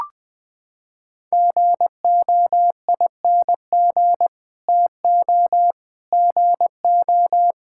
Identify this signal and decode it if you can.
Morse code, heard as a keyed tone, about 21 wpm, 700 Hz. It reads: GOING TO GO
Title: Sets of 8 Words - 20wpm